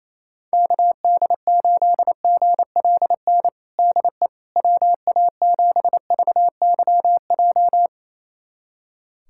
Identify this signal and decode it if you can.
Morse code, heard as a keyed tone, about 28 wpm, 700 Hz. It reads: KD8GLN DE WA7VYJ
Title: USA Callsign DE Callsign 28wpm